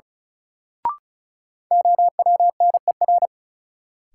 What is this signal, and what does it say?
Morse code, heard as a keyed tone, about 36 wpm, 700 Hz. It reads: OWNER
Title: Sentences from Top 1600 Words - New Word Drill Alt - 35wpm